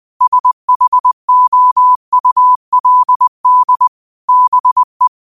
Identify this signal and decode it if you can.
Morse code, heard as a keyed tone, about 20 wpm, 1000 Hz. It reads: SHOULD BE
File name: qotdfiles / 20wpm